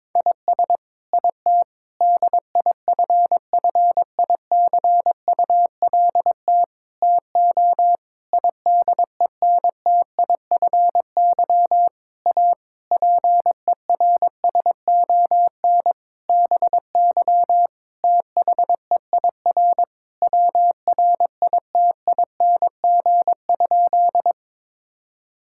Morse code, heard as a keyed tone, about 22 wpm, 700 Hz. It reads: IS IT DIFFICULT TO IDENTIFY A PERSON BY THEIR WRITING?
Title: Sentences from Top 1000 Words - Review Part 6 - 22wpm